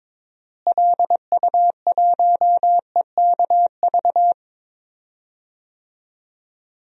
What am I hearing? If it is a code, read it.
Morse code, heard as a keyed tone, about 22 wpm, 700 Hz. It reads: LU1EKV
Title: DXCC-100 Call Signs - 6 Characters - Repeated 3x - 22wpm